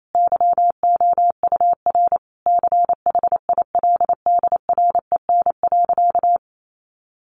Morse code, heard as a keyed tone, about 28 wpm, 700 Hz. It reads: YOUR CHILDREN.
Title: Sentences from Top 1400 Words 28wpm